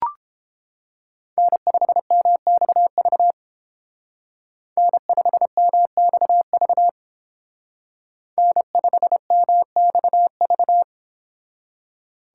Morse code, heard as a keyed tone, about 30 wpm, 700 Hz. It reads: N5MXV N5MXV N5MXV